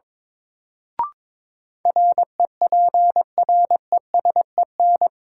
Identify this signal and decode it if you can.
Morse code, heard as a keyed tone, about 22 wpm, 700 Hz. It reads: REPRESEN
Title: Sets of 5 Words Encore - New Word Drill - 22wpm